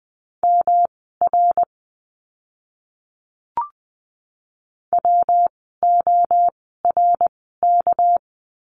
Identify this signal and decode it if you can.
Morse code, heard as a keyed tone, about 20 wpm, spaced as for 14 wpm, 700 Hz. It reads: MR WORK
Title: 244 Lesson 39f - Letter z - Words 20wpm